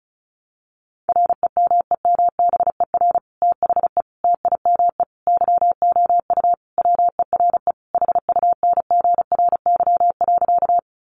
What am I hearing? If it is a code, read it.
Morse code, heard as a keyed tone, about 34 wpm, 700 Hz. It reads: REMEMBER THE TIME YOU WERE HUNGRY.
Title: Sentences from Top 1400 Words 35wpm